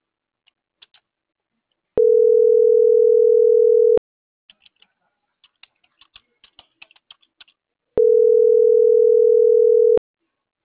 Ringback tone (North America), 440 + 480 Hz.